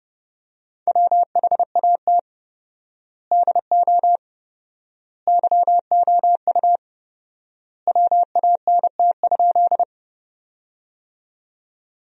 Morse code, heard as a keyed tone, about 32 wpm, 700 Hz. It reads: WHAT DO YOU WANT?
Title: 4x Word-Spacing - Sentences from Top 200 Words - 30wpm